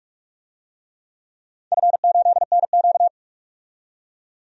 Morse code, heard as a keyed tone, about 44 wpm, 700 Hz. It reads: F8NQ
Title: DXCC-100 Call Signs - 4 Characters - 45wpm